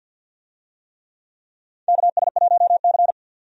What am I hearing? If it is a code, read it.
Morse code, heard as a keyed tone, about 47 wpm, 700 Hz. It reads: KS1C
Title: USA Call Signs - 4 Characters - 50wpm